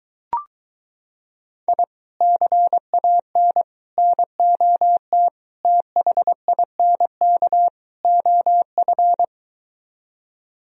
Morse code, heard as a keyed tone, about 23 wpm, 700 Hz. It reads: I CAN NOT THINK OF